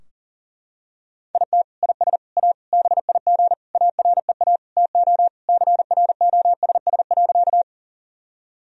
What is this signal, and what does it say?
Morse code, heard as a keyed tone, about 40 wpm, 700 Hz. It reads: IT IS A BIG AREA TO CROSS.
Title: Sentences from Top 1000 Words - Review Part 2 - 40wpm